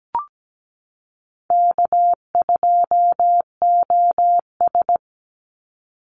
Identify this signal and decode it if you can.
Morse code, heard as a keyed tone, about 17 wpm, 700 Hz. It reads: K2OS
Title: Call Signs Repeated 3x - 17wpm